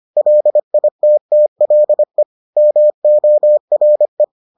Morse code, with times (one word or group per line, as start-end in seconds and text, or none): 0.2-2.2 LITTLE
2.6-4.2 MORE